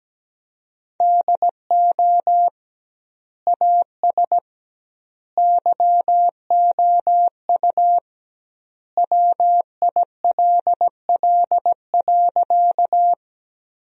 Morse code, heard as a keyed tone, about 17 wpm, 700 Hz. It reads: DO AS YOU WILL.
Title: 2x Word-Spacing - Sentences from Top 100 Words - 17wpm